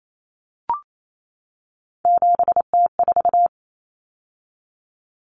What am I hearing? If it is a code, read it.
Morse code, heard as a keyed tone, about 28 wpm, 700 Hz. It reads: 7T4
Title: Common Cut Numbers - 3-Digit 28wpm